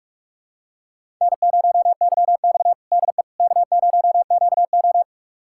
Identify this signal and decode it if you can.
Morse code, heard as a keyed tone, about 46 wpm, 700 Hz. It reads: N0YX DE K0QO
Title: USA Callsign DE Callsign - 4 Characters - 45wpm